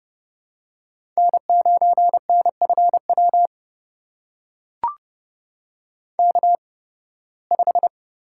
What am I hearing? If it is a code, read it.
Morse code, heard as a keyed tone, about 32 wpm, spaced as for 20 wpm, 700 Hz. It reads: N9NFW K5
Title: USA Call Signs - 5 Characters - 8x Character Spacing - 30wpm